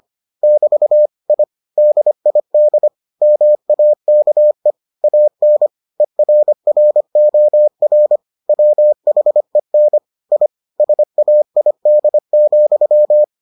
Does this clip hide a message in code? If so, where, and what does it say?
Yes: Morse code from 0.4 to 13.2 s, heard as a keyed tone, about 25 wpm, 600 Hz. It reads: <BT> I DID MAKE AN ERROR WHEN I SAID,